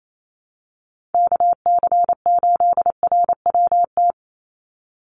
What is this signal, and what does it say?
Morse code, heard as a keyed tone, about 28 wpm, 700 Hz. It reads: KC8RWT
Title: USA Call Signs 28wpm